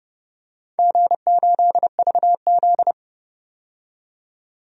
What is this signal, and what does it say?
Morse code, heard as a keyed tone, about 31 wpm, 700 Hz. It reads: G8VZ
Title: DXCC-100 Call Signs - 4 Characters - 30wpm